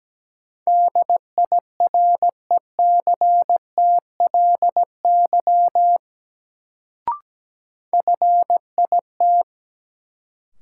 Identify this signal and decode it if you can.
Morse code, heard as a keyed tone, about 17 wpm, 700 Hz. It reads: DIRECTLY FIT
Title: Sentences from Top 800 Words - New Word Drill - 17wpm